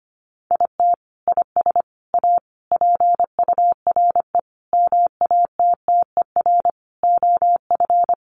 Morse code, heard as a keyed tone, about 25 wpm, 700 Hz. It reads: IT IS A PURE MATTER OF
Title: Sentences from Top 1500 Words 25wpm